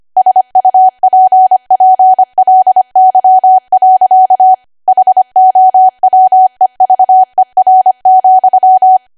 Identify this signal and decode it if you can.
Morse code, heard as a keyed tone, about 25 wpm, 750 Hz. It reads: SUPPLY. HOWEVER,